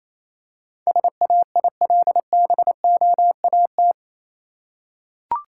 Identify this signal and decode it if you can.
Morse code, heard as a keyed tone, about 27 wpm, 700 Hz. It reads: SAILBOAT
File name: Easy Words - Speed-Racing with Hints - 22wpm